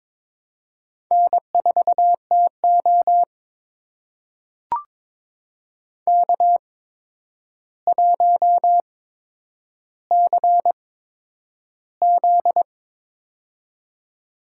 Morse code, heard as a keyed tone, about 22 wpm, spaced as for 9 wpm, 700 Hz. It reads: N4TO K1CZ
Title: USA Call Signs - 4 Characters - 8x Character Spacing - 22wpm